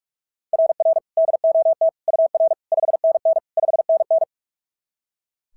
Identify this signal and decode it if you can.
Morse code, heard as a keyed tone, about 44 wpm, 690 Hz. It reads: RR DOT UR 5NN 5NN
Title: SOTA QSO Practice 45wpm